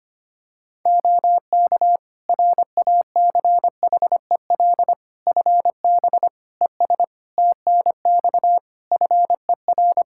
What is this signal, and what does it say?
Morse code, heard as a keyed tone, about 25 wpm, 700 Hz. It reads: OK RACHEL FB ES TNX FER
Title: Practice QSO - Exchange 2 - Info - 25wpm